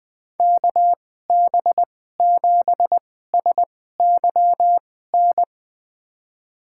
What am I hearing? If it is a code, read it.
Morse code, heard as a keyed tone, about 20 wpm, spaced as for 15 wpm, 700 Hz. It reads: KB7SYN
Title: USA Call Signs - 6 Characters - 2x Character Spacing - 20wpm